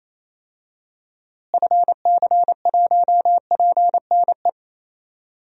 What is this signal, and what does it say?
Morse code, heard as a keyed tone, about 28 wpm, 700 Hz. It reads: FC1PNE
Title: DXCC-100 Call Signs - 6 Characters - Repeated 2x - 28wpm